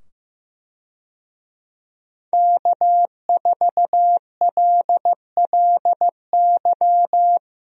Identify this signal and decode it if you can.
Morse code, heard as a keyed tone, about 15 wpm, 700 Hz. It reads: K4LLY
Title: USA Call Signs - 5 Characters - 15wpm